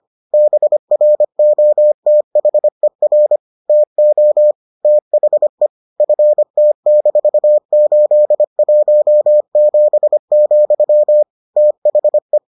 Morse code, heard as a keyed tone, about 25 wpm, 600 Hz. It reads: BROTHER TO THE FT-817, THE